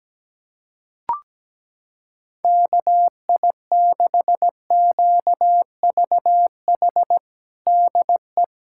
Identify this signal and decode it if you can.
Morse code, heard as a keyed tone, about 17 wpm, 700 Hz. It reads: KI6QVH DE
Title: USA Callsign DE Callsign 17wpm